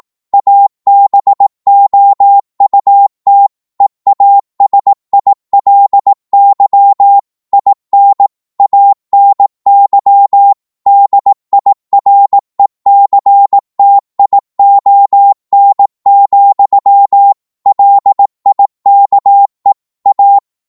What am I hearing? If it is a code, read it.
Morse code, heard as a keyed tone, about 18 wpm, 800 Hz. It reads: ABOUT EASILY IN ANY DIRECTION, LIKE A
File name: 18wpm